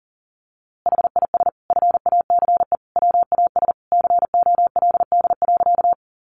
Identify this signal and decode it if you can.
Morse code, heard as a keyed tone, about 40 wpm, 700 Hz. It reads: HIS FACE WAS COLD.
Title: A Sound Of Thunder 40wpm